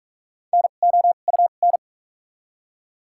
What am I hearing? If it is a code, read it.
Morse code, heard as a keyed tone, about 46 wpm, spaced as for 33 wpm, 700 Hz. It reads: NOUN
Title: 2x Character Spacing - Top 500 Words - 45wpm